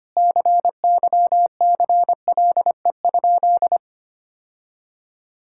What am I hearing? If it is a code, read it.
Morse code, heard as a keyed tone, about 25 wpm, 700 Hz. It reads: CYCLE?
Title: Sentences from Top 1000 Words - Review Part 6 - 25wpm